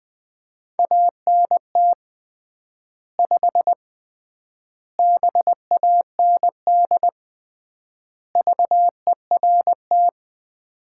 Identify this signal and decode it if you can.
Morse code, heard as a keyed tone, about 20 wpm, 700 Hz. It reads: ANT 5 BAND VERT
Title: Common QSO Phrases - 3x Word-Spacing - 20wpm